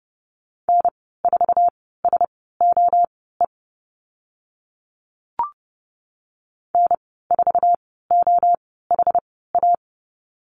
Morse code, heard as a keyed tone, about 30 wpm, spaced as for 17 wpm, 700 Hz. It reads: N4SOE N4OHA
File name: Lesson 14g - Number 2 - Call Signs 30wpm